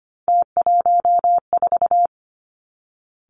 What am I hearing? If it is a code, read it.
Morse code, heard as a keyed tone, about 25 wpm, 700 Hz. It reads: T14
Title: Three Challenging Letter-Number Combinations 25wpm